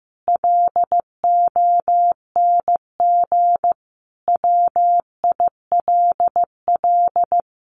Morse code, heard as a keyed tone, about 15 wpm, 700 Hz. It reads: LONG WILL
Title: Sentences from Top 1200 Words 15wpm